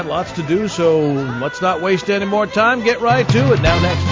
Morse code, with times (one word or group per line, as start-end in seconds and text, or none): none